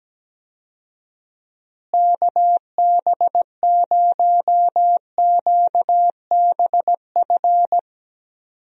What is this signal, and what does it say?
Morse code, heard as a keyed tone, about 17 wpm, 700 Hz. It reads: KB0QBF
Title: USA Call Signs - 6 Characters - Repeated 2x - 17wpm